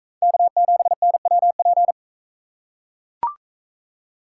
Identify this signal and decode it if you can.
Morse code, heard as a keyed tone, about 40 wpm, 700 Hz. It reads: K7NWP